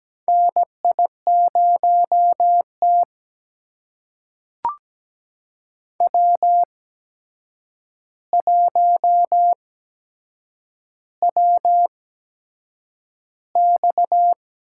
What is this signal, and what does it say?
Morse code, heard as a keyed tone, about 17 wpm, spaced as for 7 wpm, 700 Hz. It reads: NI0T W1WX
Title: USA Call Signs - 4 Characters - 8x Character Spacing - 17wpm